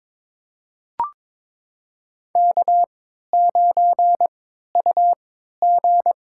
Morse code, heard as a keyed tone, about 21 wpm, spaced as for 12 wpm, 700 Hz. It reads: K9UG
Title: USA Call Signs - 4 Characters - 3x Character Spacing - 22wpm